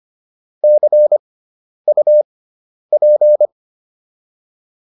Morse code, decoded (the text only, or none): CUP